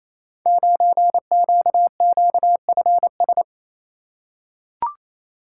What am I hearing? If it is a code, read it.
Morse code, heard as a keyed tone, about 28 wpm, 700 Hz. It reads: 9QQFS
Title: Code Groups - 5 Letter:Numbers - 28wpm